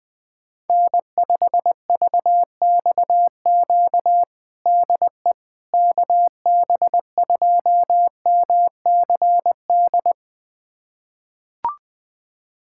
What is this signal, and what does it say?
Morse code, heard as a keyed tone, about 20 wpm, 700 Hz. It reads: N5VXQ DE KB2MCD
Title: USA Callsign DE Callsign 20wpm